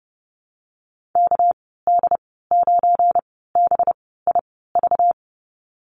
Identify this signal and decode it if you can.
Morse code, heard as a keyed tone, about 29 wpm, spaced as for 17 wpm, 700 Hz. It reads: KD9BIV
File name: Lesson 37g - Letter x - Call Signs 30wpm